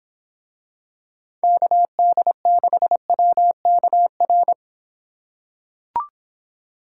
Morse code, heard as a keyed tone, about 26 wpm, 700 Hz. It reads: KD6WKR